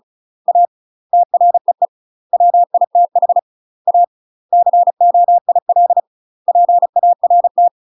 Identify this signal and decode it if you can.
Morse code, heard as a keyed tone, about 35 wpm, 700 Hz. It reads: A TREE WITH A COIL PART
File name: UK MASTER QSOs TRAINING / 35WPM / UK MASTER QSOs 0000